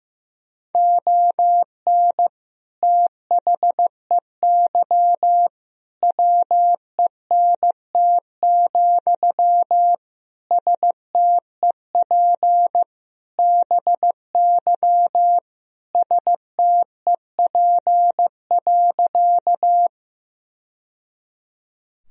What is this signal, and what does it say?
Morse code, heard as a keyed tone, about 15 wpm, 700 Hz. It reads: ON THEY WENT, STEP BY STEP.
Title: Sentences from Top 1000 Words - Review Part 2 - 15wpm